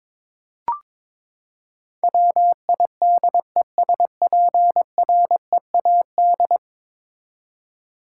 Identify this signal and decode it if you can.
Morse code, heard as a keyed tone, about 22 wpm, 700 Hz. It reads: WIDESPREAD
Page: morse-code-ninja-practice-22wpm